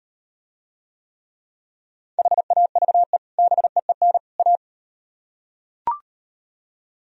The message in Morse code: HAVE BEEN A

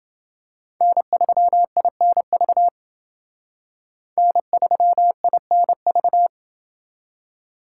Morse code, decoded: N3INV N3INV

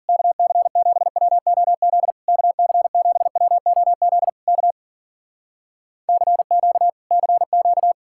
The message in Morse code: KC7WOZ KC7WOZ K CQ CQ

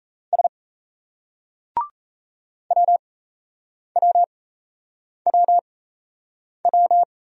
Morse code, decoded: S WWWW

